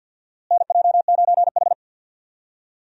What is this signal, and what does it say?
Morse code, heard as a keyed tone, about 47 wpm, 700 Hz. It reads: NJ9H
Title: USA Call Signs - 4 Characters - Repeated 3x - 50wpm